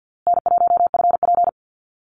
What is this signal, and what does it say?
Morse code, heard as a keyed tone, about 50 wpm, 700 Hz. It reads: N1FP
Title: USA Call Signs Repeated 3x 50wpm